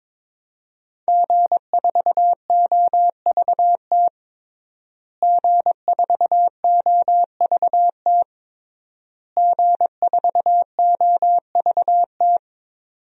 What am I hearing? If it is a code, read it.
Morse code, heard as a keyed tone, about 22 wpm, 700 Hz. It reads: G4OVT G4OVT G4OVT
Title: DXCC-100 Call Signs - 5 Characters - Repeated 3x - 22wpm